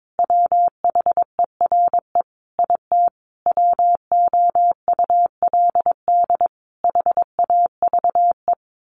Morse code, heard as a keyed tone, about 22 wpm, 700 Hz. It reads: WHERE IT WOULD HAVE
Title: A Sound Of Thunder 22wpm